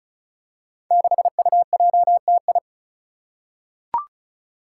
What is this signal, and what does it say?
Morse code, heard as a keyed tone, about 36 wpm, 700 Hz. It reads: 6UJTI
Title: Five Challenging Letter-Number Combinations 35wpm